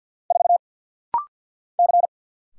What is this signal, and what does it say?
Morse code, heard as a keyed tone, about 50 wpm, 690 Hz. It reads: 4 C
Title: Single Letter-Number - Rapid-Fire 50wpm